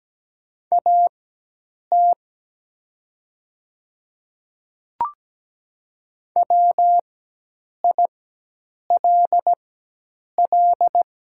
Morse code, heard as a keyed tone, about 17 wpm, spaced as for 8 wpm, 700 Hz. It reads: AT WILL
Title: 4x Character Spacing - Top 100 Words - 17wpm